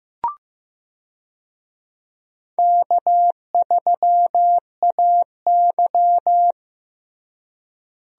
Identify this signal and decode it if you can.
Morse code, heard as a keyed tone, about 15 wpm, 700 Hz. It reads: K3AY
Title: USA Call Signs - 4 Characters - Repeated 2x - 15wpm